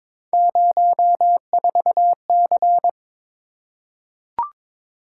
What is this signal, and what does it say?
Morse code, heard as a keyed tone, about 21 wpm, 700 Hz. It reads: 04C
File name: practice / 22wpm / Three Letter-Number Combinations - 22wpm